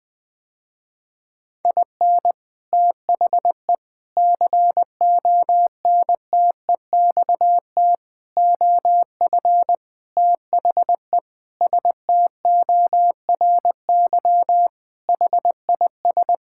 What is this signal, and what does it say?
Morse code, heard as a keyed tone, about 20 wpm, 700 Hz. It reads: IN THE CONTEXT OF THE STORY HIS
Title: Sentences with 5-Word Phrases - Spoken Beginning - 20wpm